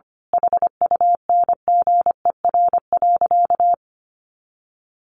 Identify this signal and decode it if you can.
Morse code, heard as a keyed tone, about 25 wpm, 700 Hz. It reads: HUNGER.
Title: A Sound Of Thunder 25wpm